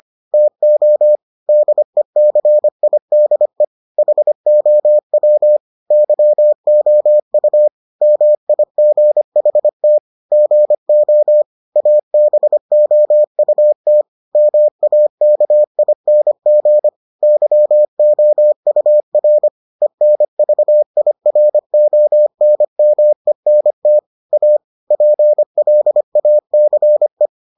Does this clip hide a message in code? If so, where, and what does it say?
Yes: Morse code from 0.3 to 27.3 s, heard as a keyed tone, about 25 wpm, 600 Hz. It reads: TO DECIDE HOW YOU MIGHT GO ABOUT MAKING YOUR ENVIRONMENT A PLACE